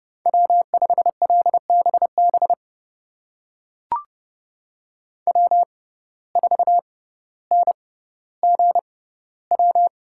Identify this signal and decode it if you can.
Morse code, heard as a keyed tone, about 30 wpm, spaced as for 15 wpm, 700 Hz. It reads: W5LBB W4NGW
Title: USA Call Signs - 5 Characters - 6x Character Spacing - 30wpm